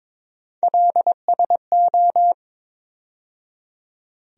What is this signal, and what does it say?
Morse code, heard as a keyed tone, about 21 wpm, 700 Hz. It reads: LSO
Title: USA Call Sign Postfixes - 3 Letters 22wpm